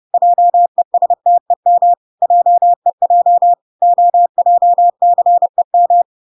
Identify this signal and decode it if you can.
Morse code, heard as a keyed tone, about 30 wpm, 700 Hz. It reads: JESTEM JEJ OJCEM